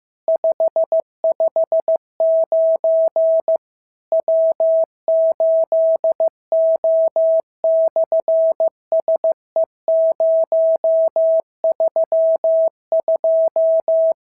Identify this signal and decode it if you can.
Morse code, heard as a keyed tone, about 15 wpm, 650 Hz. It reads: 559 W8O/SE032
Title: SOTA QSO Practice 15wpm